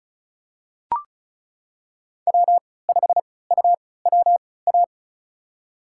Morse code, heard as a keyed tone, about 36 wpm, spaced as for 20 wpm, 700 Hz. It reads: W5UWA
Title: USA Call Signs - 5 Characters - 3x Character Spacing - 35wpm